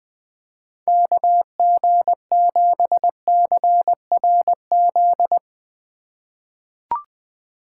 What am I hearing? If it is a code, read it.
Morse code, heard as a keyed tone, about 20 wpm, 700 Hz. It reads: KG7CRZ
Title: USA Call Signs - 4x Character Spacing - 20wpm